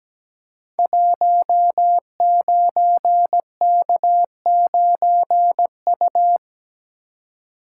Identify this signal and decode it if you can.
Morse code, heard as a keyed tone, about 17 wpm, 700 Hz. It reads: 19K9U
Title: Code Groups - 5 Letter:Numbers - 17wpm